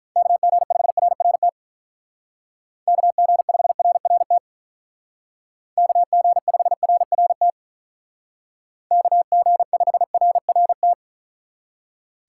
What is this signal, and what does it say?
Morse code, heard as a keyed tone, about 43 wpm, 700 Hz. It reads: KG5RRT KG5RRT KG5RRT KG5RRT
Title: Speed Racing - USA Call Signs 35wpm